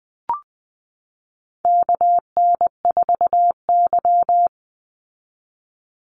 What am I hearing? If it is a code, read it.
Morse code, heard as a keyed tone, about 20 wpm, 700 Hz. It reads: KN4Y
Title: Call Signs Repeated 3x - 20wpm